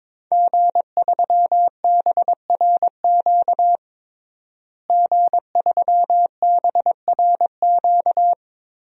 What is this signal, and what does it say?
Morse code, heard as a keyed tone, about 22 wpm, 700 Hz. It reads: G3BRQ G3BRQ
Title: DXCC-100 Call Signs - 5 Characters - Repeated 2x - 22wpm